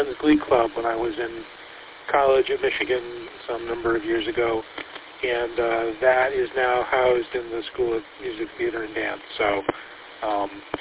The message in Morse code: none

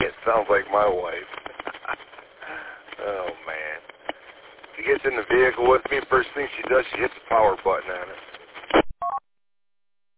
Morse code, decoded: none

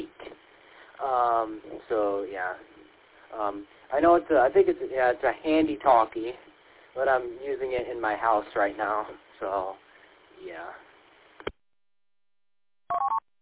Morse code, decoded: none

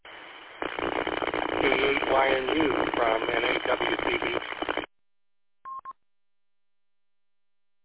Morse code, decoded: none

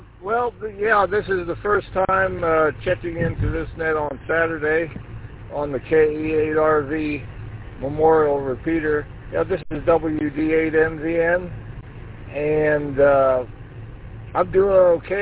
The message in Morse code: none

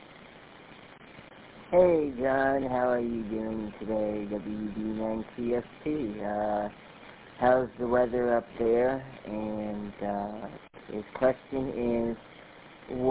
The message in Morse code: none

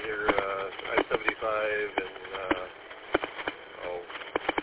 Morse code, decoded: none

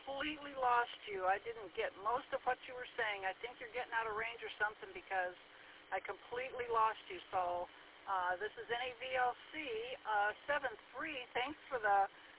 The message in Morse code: none

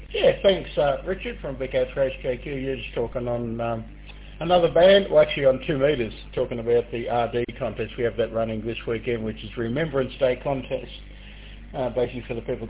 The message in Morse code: none

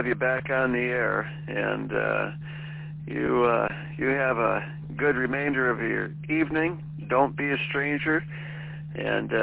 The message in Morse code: none

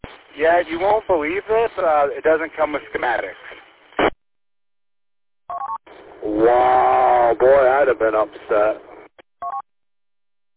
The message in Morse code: none

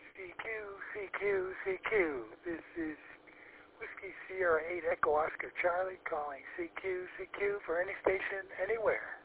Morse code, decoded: none

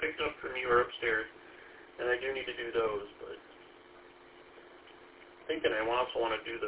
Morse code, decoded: none

